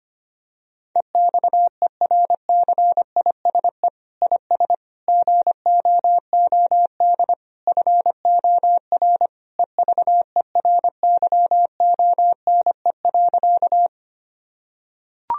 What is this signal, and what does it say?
Morse code, heard as a keyed tone, about 25 wpm, 700 Hz. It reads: EXERCISE IS GOOD FOR EVERYONE.
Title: Sentences from Top 1000 Words - Review Part 4 - 25wpm